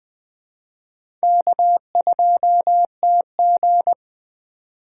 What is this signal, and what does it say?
Morse code, heard as a keyed tone, about 21 wpm, 700 Hz. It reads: K2TG
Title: USA Call Signs - 4 Characters - 2x Character Spacing - 20wpm